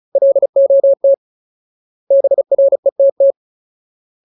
Morse code, heard as a keyed tone, about 34 wpm, 550 Hz. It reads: LOT BRETT
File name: Ultimate Word List / 35WPM Ultimate